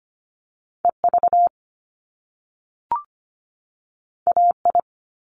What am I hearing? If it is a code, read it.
Morse code, heard as a keyed tone, about 25 wpm, 700 Hz. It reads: EV AI
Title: Top Two Letter Combinations in Words 25wpm